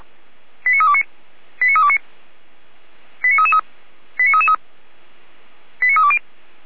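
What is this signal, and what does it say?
Five-tone selective call (ZVEI), heard as quick runs of five tones, tones of about 70 ms. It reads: 89329 89329 89333 89333 89320
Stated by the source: Received on an SDR; ZVEI five-tone call.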